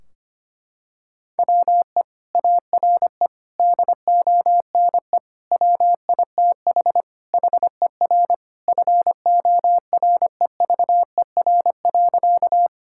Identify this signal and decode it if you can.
Morse code, heard as a keyed tone, about 25 wpm, 700 Hz. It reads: WE ARE DONE WITH HER FOREVER.
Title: Sentences from Top 1600 Words 25wpm